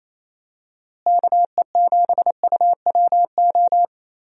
Morse code, heard as a keyed tone, about 27 wpm, 700 Hz. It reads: KE7UWO
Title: USA Call Signs - 6 Characters - Repeated 3x - 28wpm